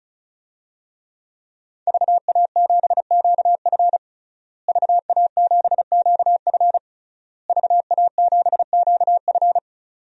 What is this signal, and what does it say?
Morse code, heard as a keyed tone, about 36 wpm, 700 Hz. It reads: VA7QF VA7QF VA7QF
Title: DXCC-100 Call Signs - 5 Characters - Repeated 3x - 35wpm